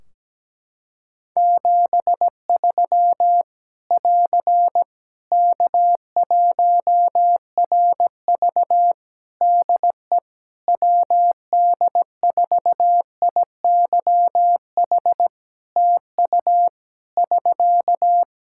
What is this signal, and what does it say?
Morse code, heard as a keyed tone, about 17 wpm, 700 Hz. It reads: 73 <AR> K1RV DE WD4IYH TU <SK>
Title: Practice QSO - Exchange 4 - Ending - 17wpm